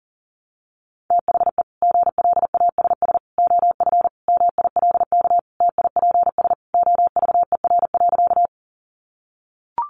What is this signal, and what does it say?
Morse code, heard as a keyed tone, about 40 wpm, 700 Hz. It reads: THE GLASS OF MILK TIPS OVER.
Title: Additional Easy Phrases 40wpm